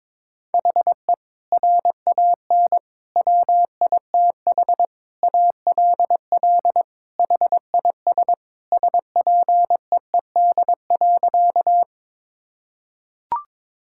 Morse code, heard as a keyed tone, about 22 wpm, 700 Hz. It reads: HE RAN WITH ALL HIS SPEED.
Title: Sentences from Top 1000 Words - Review Part 5 - 22wpm